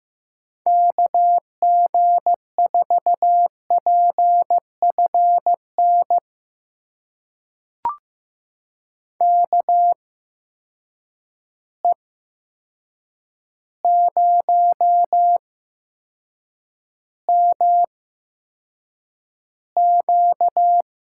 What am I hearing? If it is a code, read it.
Morse code, heard as a keyed tone, about 15 wpm, spaced as for 7 wpm, 700 Hz. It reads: KG4PFN KE0MQ
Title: USA Call Signs - 6 Characters - 8x Character Spacing - 15wpm